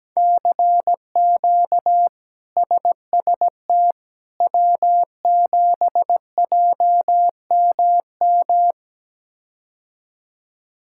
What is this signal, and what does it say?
Morse code, heard as a keyed tone, about 17 wpm, 700 Hz. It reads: CQ SST W7JMM
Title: K1USN SST QSO Practice - 17wpm